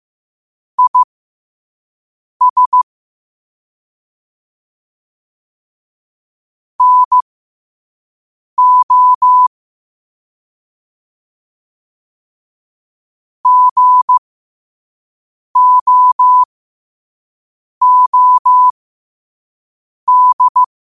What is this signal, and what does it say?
Morse code, heard as a keyed tone, about 15 wpm, spaced as for 5 wpm, 1000 Hz. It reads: IS NO GOOD
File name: qotdfiles / 5wpm